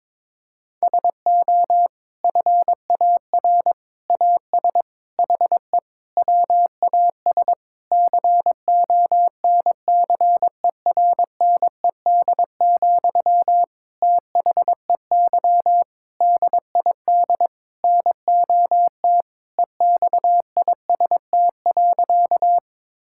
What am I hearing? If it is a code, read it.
Morse code, heard as a keyed tone, about 22 wpm, 700 Hz. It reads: SO FAR AS HE WAS CONCERNED, THEY DID NOT EXIST.
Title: Sentences from Top 1000 Words - Review Part 6 - 22wpm